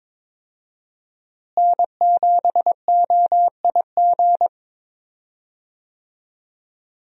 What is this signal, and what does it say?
Morse code, heard as a keyed tone, about 22 wpm, 700 Hz. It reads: N7OIG
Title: USA Call Signs - 5 Characters - Repeated 3x - 22wpm